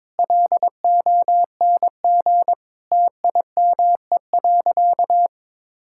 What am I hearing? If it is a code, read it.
Morse code, heard as a keyed tone, about 22 wpm, 700 Hz. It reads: LONG TIME.
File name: Sentences with 7-Word Phrases - 22wpm